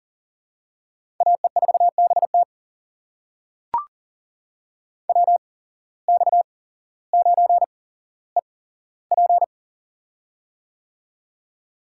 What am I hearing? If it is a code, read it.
Morse code, heard as a keyed tone, about 39 wpm, spaced as for 17 wpm, 700 Hz. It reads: AE4BT WX9EP
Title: USA Call Signs - 5 Characters - 8x Character Spacing - 40wpm